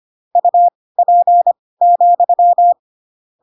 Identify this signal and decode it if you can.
Morse code, heard as a keyed tone, about 25 wpm, spaced as for 18 wpm, 700 Hz. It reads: UP,
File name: Pinocchio ch3 12 25 ik1hge